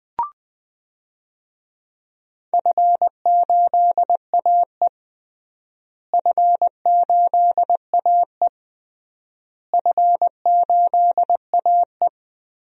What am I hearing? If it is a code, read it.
Morse code, heard as a keyed tone, about 20 wpm, 700 Hz. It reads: F8AE F8AE F8AE
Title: DXCC-100 Call Signs - 4 Characters - Repeated 3x - 20wpm